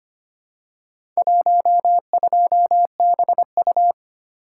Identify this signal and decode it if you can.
Morse code, heard as a keyed tone, about 25 wpm, 700 Hz. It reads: 12BU
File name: practice / 25wpm / Four Letter-Number Combinations - 25wpm